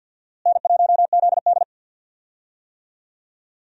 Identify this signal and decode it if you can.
Morse code, heard as a keyed tone, about 50 wpm, 700 Hz. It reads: N1ZD
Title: USA Call Signs - 4 Characters - Repeated 2x - 50wpm